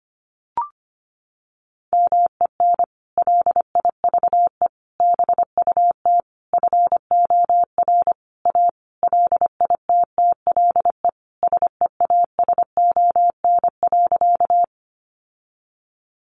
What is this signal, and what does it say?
Morse code, heard as a keyed tone, about 25 wpm, 700 Hz. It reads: MEN LIVE BUT FOR A LITTLE SEASON.
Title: Sentences from Top 1400 Words 25wpm